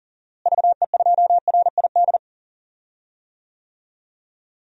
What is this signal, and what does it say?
Morse code, heard as a keyed tone, about 40 wpm, 700 Hz. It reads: VE2RID
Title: DXCC-100 Call Signs - 6 Characters - Repeated 3x - 40wpm